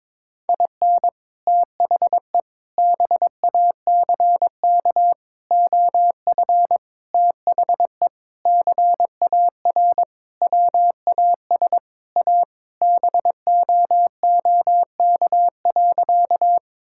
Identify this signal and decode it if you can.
Morse code, heard as a keyed tone, about 22 wpm, 700 Hz. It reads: IN THE BACK OF THE CAR WAS A BOOK.
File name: practice / 22wpm / Sentences with 4-Word Phrases - 22wpm